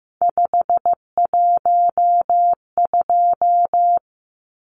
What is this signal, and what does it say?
Morse code, heard as a keyed tone, about 15 wpm, 700 Hz. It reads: 512
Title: Numbers - 3-Digit 15wpm